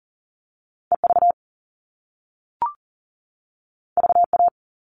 Morse code, heard as a keyed tone, about 40 wpm, 700 Hz. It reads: EV VA
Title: All Cut Numbers - 2-Digit 40wpm